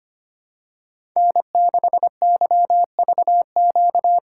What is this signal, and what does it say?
Morse code, heard as a keyed tone, about 25 wpm, 700 Hz. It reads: N6YVQ
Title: USA Call Signs - 5 Characters - 4x Character Spacing - 25wpm